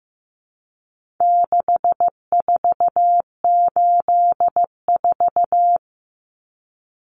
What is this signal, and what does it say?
Morse code, heard as a keyed tone, about 15 wpm, 700 Hz. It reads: 6484
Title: Numbers - 4-Digit 15wpm